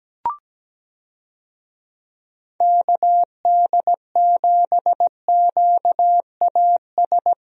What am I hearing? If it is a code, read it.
Morse code, heard as a keyed tone, about 17 wpm, 700 Hz. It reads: KD7QAS
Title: USA Call Signs - 6 Characters - 17wpm